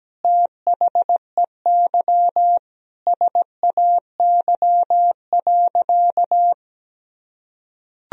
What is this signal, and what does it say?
Morse code, heard as a keyed tone, about 17 wpm, 700 Hz. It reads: THEY SAY.
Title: Sentences from Top 1000 Words - Review Part 1 - 17wpm